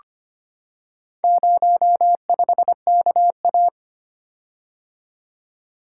Morse code, heard as a keyed tone, about 25 wpm, 700 Hz. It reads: 05KA
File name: practice / 25wpm / Four Letter-Number Combinations - 25wpm